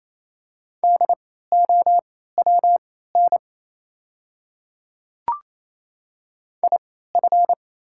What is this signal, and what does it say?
Morse code, heard as a keyed tone, about 28 wpm, spaced as for 16 wpm, 690 Hz. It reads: DOWN IF
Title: 3x Character Spacing - Top 100 Words - 28wpm